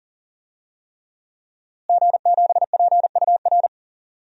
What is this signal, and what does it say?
Morse code, heard as a keyed tone, about 39 wpm, 700 Hz. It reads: G7PUR
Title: DXCC-100 Call Signs - 5 Characters - Repeated 3x - 40wpm